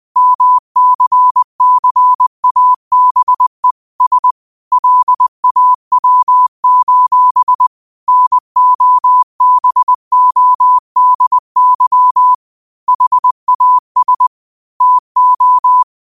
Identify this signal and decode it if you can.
Morse code, heard as a keyed tone, about 20 wpm, 1000 Hz. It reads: MCCABE S LAW: NOBODY HAS TO
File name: 20wpm